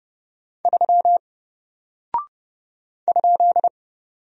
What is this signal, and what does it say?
Morse code, heard as a keyed tone, about 30 wpm, 700 Hz. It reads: 3 ?
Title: Single Character (Letter:Number:Punctuation:Prosign) Alt 30wpm